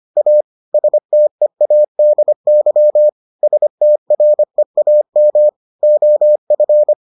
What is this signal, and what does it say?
Morse code, heard as a keyed tone, about 25 wpm, 600 Hz. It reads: A STEADY STREAM OF